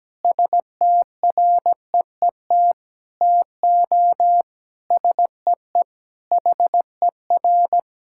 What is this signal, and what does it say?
Morse code, heard as a keyed tone, about 17 wpm, 700 Hz. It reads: STREET TO SEE HER
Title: Sentences from Top 1000 Words - Review Part 5 - 17wpm